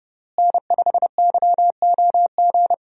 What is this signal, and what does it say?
Morse code, heard as a keyed tone, about 30 wpm, 700 Hz. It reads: N5YOG
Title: USA Call Signs - 5 Characters - Repeated 2x - 30wpm